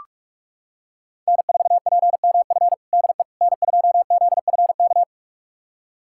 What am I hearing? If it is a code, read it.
Morse code, heard as a keyed tone, about 43 wpm, 700 Hz. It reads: N4PMF DE N2ZFK
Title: USA Callsign DE Callsign - 5 Characters - 45wpm